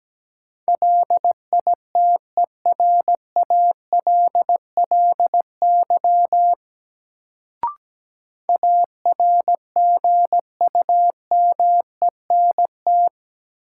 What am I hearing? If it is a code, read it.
Morse code, heard as a keyed tone, about 17 wpm, 700 Hz. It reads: LITERALLY ARGUMENT
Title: Sentences from Top 1400 Words - New Word Drill Alt - 17wpm